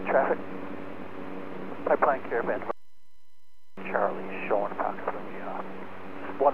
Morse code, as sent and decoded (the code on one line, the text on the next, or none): none